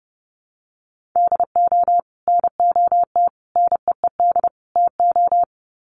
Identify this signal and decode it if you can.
Morse code, heard as a keyed tone, about 30 wpm, 700 Hz. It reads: DO NOT NEED TO
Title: Sets of 4 Words 30wpm